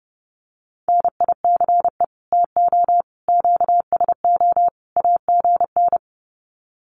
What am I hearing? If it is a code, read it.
Morse code, heard as a keyed tone, about 30 wpm, 700 Hz. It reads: NICE TO QSO AGN
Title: Common QSO Phrases 30wpm